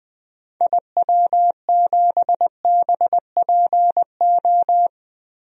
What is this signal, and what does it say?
Morse code, heard as a keyed tone, about 21 wpm, 700 Hz. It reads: IW7BPO
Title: DXCC-100 Call Signs - 6 Characters - 20wpm